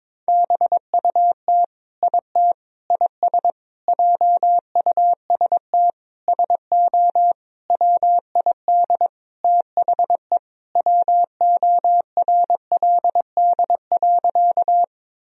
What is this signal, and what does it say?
Morse code, heard as a keyed tone, about 22 wpm, 700 Hz. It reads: BUT IT IS JUST SO WID THE WORLD.
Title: Sentences from Top 200 Words and Abbreviations 22wpm